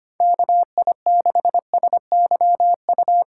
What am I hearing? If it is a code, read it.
Morse code, heard as a keyed tone, about 28 wpm, 700 Hz. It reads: KI6SYU
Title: Speed Racing - USA 2x3 Call Signs 25wpm